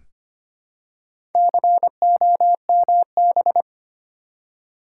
Morse code, heard as a keyed tone, about 25 wpm, 700 Hz. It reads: COMB
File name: practice / 25wpm / Easy Words with Hints - 25wpm